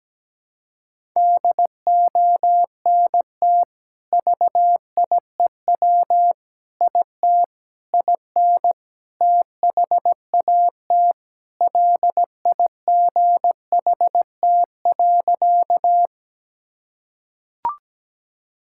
Morse code, heard as a keyed tone, about 17 wpm, 700 Hz. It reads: DONT VIEW IT IN THAT LIGHT.
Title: Sentences from Top 1000 Words - Review Part 6 - 17wpm